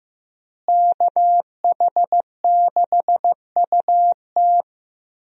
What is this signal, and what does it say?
Morse code, heard as a keyed tone, about 15 wpm, 700 Hz. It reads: KH6UT
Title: USA Call Signs - 5 Characters - Repeated 2x - 15wpm